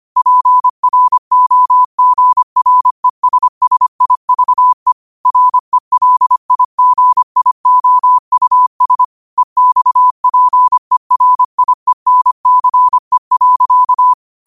Morse code, heard as a keyed tone, about 25 wpm, 1000 Hz. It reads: PROGRESSIVE RELIGIOUS EXPERIENCE.
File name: qotdfiles / 25wpm